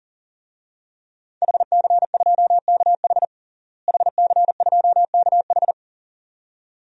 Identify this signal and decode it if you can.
Morse code, heard as a keyed tone, about 40 wpm, 700 Hz. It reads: HC2KH HC2KH
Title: DXCC-100 Call Signs - 5 Characters - Repeated 2x - 40wpm